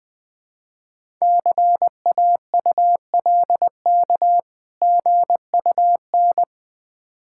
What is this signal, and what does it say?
Morse code, heard as a keyed tone, about 20 wpm, 690 Hz. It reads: CAULK GUN
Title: Difficult Words with Hints - 20wpm